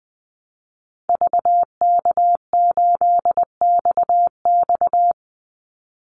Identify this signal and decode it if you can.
Morse code, heard as a keyed tone, about 20 wpm, 700 Hz. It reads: VK8XX
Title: Call Signs Repeated 2x - 20wpm